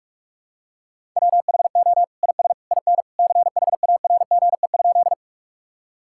Morse code, heard as a keyed tone, about 46 wpm, 700 Hz. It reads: WHO IS IN CHARGE?
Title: Sentences from Top 1000 Words - Review Part 6 - 45wpm